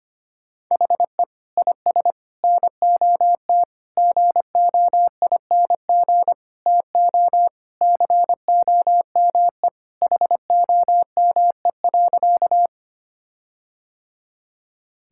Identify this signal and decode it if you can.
Morse code, heard as a keyed tone, about 25 wpm, 700 Hz. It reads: HE IS NOT GOING TO COME HOME.
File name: practice / 25wpm / Sentences with 4-Word Phrases - 25wpm